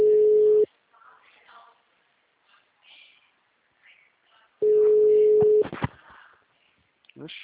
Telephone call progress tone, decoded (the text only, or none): ringback tone